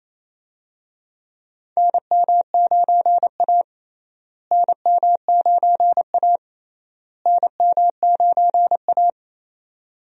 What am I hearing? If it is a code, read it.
Morse code, heard as a keyed tone, about 28 wpm, 700 Hz. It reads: NM9A NM9A NM9A
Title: USA Call Signs - 4 Characters - Repeated 3x - 28wpm